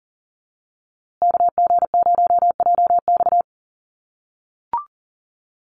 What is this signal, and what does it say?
Morse code, heard as a keyed tone, about 40 wpm, 700 Hz. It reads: KG0JX
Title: USA Call Signs Repeated 3x 40wpm